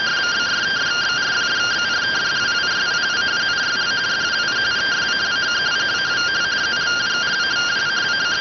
Two-tone teleprinter signal, about 170 Hz shift.